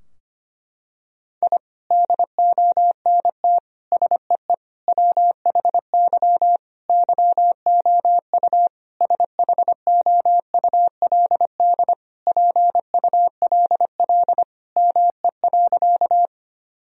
Morse code, heard as a keyed tone, about 25 wpm, 700 Hz. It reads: I DONT SEE WHY YOU SHOULD PULL ME.
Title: Sentences from Top 1000 Words - Review Part 2 - 25wpm